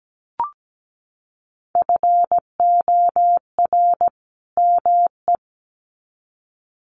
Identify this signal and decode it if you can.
Morse code, heard as a keyed tone, about 17 wpm, 700 Hz. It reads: FOR ME
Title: Sets of 2 Words 17wpm